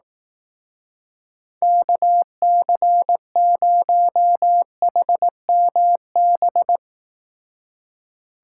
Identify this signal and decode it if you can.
Morse code, heard as a keyed tone, about 18 wpm, 700 Hz. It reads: KC0HMB
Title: Speed Racing - USA 2x3 Call Signs 15wpm